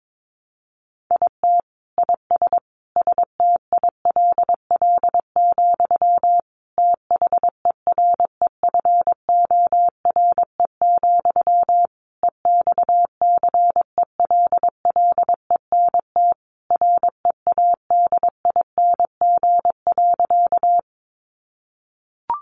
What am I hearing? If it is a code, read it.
Morse code, heard as a keyed tone, about 22 wpm, 700 Hz. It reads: IT IS STILL, THEREFORE, EXCELLENT READING.
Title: Sentences from Top 1000 Words 22wpm